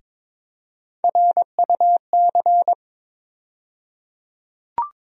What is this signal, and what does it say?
Morse code, heard as a keyed tone, about 22 wpm, 700 Hz. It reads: RUC
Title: USA Call Sign Postfixes - 3 Letters 22wpm